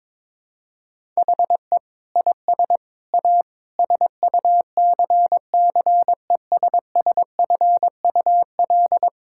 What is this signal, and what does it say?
Morse code, heard as a keyed tone, about 22 wpm, 700 Hz. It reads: HE IS A SUCCESSFUL
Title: Sentences from Top 1000 Words - Review Part 6 - 22wpm